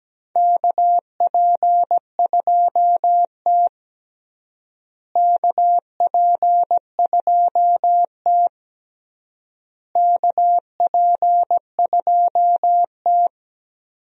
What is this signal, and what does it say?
Morse code, heard as a keyed tone, about 17 wpm, 700 Hz. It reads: KP2T KP2T KP2T
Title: DXCC-100 Call Signs - 4 Characters - Repeated 3x - 17wpm